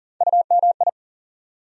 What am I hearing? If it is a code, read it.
Morse code, heard as a keyed tone, about 40 wpm, 700 Hz. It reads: UMI
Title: USA Call Sign Postfixes - 3 Letters 40wpm